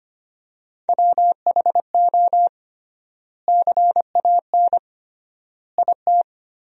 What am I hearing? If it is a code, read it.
Morse code, heard as a keyed tone, about 25 wpm, 700 Hz. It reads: WHO CAN IT
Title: 3x Word-Spacing - Sentences from Top 100 Words - 25wpm